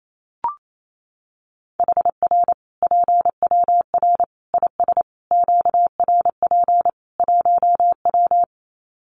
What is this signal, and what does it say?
Morse code, heard as a keyed tone, about 28 wpm, 700 Hz. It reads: HR PWR IS QRP 1W
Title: Common QSO Phrases 28wpm